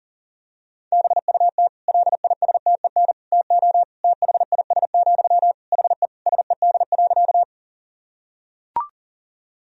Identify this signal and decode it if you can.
Morse code, heard as a keyed tone, about 40 wpm, 700 Hz. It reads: BUT LISTEN TO THIS, HE SED.
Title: Sentences from Top 400 Words and Abbreviations 40wpm